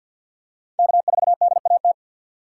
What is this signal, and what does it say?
Morse code, heard as a keyed tone, about 53 wpm, 700 Hz. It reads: K4DAT
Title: USA Call Signs - 2x Character Spacing - 50wpm